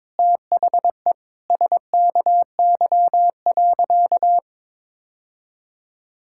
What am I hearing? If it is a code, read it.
Morse code, heard as a keyed tone, about 22 wpm, 700 Hz. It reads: THE SKY.
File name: Sentences with 9-Word Phrases - 22wpm